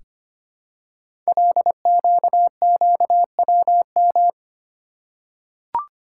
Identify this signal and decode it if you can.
Morse code, heard as a keyed tone, about 25 wpm, 700 Hz. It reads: LQQWM
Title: Code Groups - 5 Letters - 25wpm